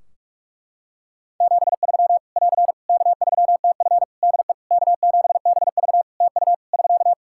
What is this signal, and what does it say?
Morse code, heard as a keyed tone, about 46 wpm, 700 Hz. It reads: 73 <AR> K3TF DE K7BV TU <SK>
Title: Practice QSO - Exchange 4 - Ending - 45wpm